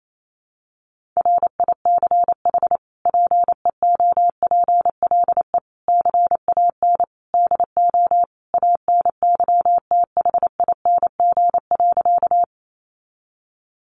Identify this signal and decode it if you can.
Morse code, heard as a keyed tone, about 28 wpm, 700 Hz. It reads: RICH PEOPLE CAN DO ANYTHING.
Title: Sentences from Top 1100 Words 28wpm